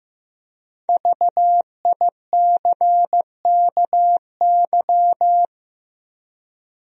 Morse code, heard as a keyed tone, about 15 wpm, 700 Hz. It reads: VICKY
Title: USA Common Names - Max 5 Letters - 15wpm